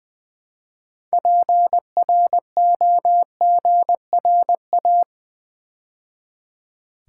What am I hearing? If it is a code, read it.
Morse code, heard as a keyed tone, about 20 wpm, 700 Hz. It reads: PROGRA
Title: Build A Word - 3 Letters At A Time - 9-Letter Words - 20wpm